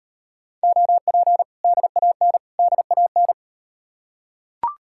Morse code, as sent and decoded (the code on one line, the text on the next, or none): --- .--. / -.. .- -. / -.. .- -.
OP DAN DAN